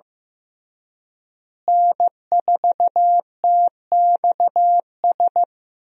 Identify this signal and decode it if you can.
Morse code, heard as a keyed tone, about 15 wpm, 700 Hz. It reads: N4TXS
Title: USA Call Signs - 5 Characters - Repeated 3x - 15wpm